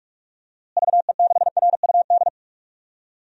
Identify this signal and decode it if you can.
Morse code, heard as a keyed tone, about 46 wpm, 700 Hz. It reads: VE6RUD
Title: DXCC-100 Call Signs - 6 Characters - Repeated 3x - 45wpm